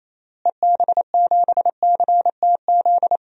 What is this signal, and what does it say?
Morse code, heard as a keyed tone, about 28 wpm, 700 Hz. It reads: EB7CTZ
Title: DXCC-100 Call Signs - 6 Characters - Repeated 3x - 28wpm